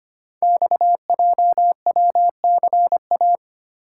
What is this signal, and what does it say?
Morse code, heard as a keyed tone, about 25 wpm, 700 Hz. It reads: XJWCA
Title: Code Groups - 5 Letter:Numbers - 25wpm